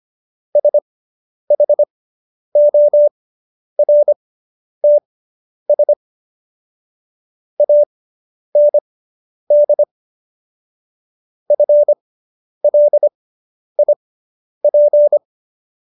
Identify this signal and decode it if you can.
Morse code, heard as a keyed tone, about 25 wpm, spaced as for 10 wpm, 590 Hz. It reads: SHORTS AND FLIP